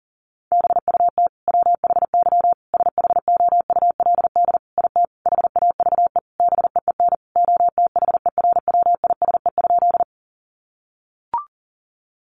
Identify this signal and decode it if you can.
Morse code, heard as a keyed tone, about 40 wpm, 700 Hz. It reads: BUT WHY SHOULD IT HAVE BEEN OTHERWISE?
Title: Sentences from Top 1200 Words 40wpm